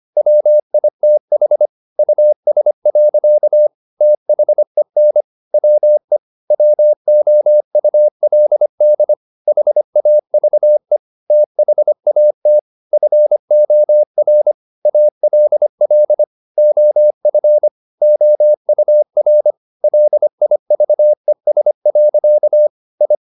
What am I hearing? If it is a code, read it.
Morse code, heard as a keyed tone, about 25 wpm, 600 Hz. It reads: WITH US. THEN WE WOULD HAVE THAT FOR ALL OF OUR LIVES. I
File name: the old man and the sea part 1 25wpm